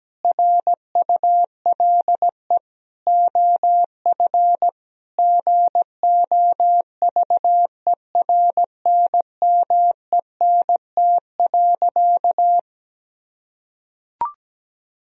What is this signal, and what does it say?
Morse code, heard as a keyed tone, about 17 wpm, 700 Hz. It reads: RULE OF GOVERNMENT.
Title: Sentences from Top 1000 Words - Review Part 3 - 17wpm